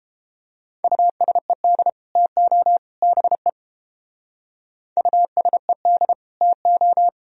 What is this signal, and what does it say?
Morse code, heard as a keyed tone, about 29 wpm, 700 Hz. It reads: USED TO BE USED TO